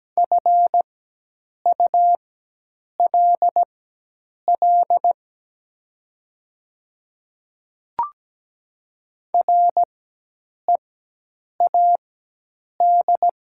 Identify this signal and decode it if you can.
Morse code, heard as a keyed tone, about 17 wpm, spaced as for 8 wpm, 700 Hz. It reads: FULL READ
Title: 4x Character Spacing - Top 500 Words - 17wpm